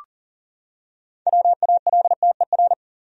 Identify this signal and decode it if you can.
Morse code, heard as a keyed tone, about 37 wpm, 700 Hz. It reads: WALTER
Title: USA Common Names - 40wpm